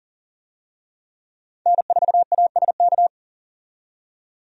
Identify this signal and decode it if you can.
Morse code, heard as a keyed tone, about 40 wpm, 700 Hz. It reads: N4ASK
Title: USA Call Signs - 5 Characters - 40wpm